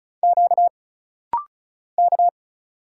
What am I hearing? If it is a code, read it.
Morse code, heard as a keyed tone, about 34 wpm, 700 Hz. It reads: Q K
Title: Single Letter-Number - Rapid-Fire 35wpm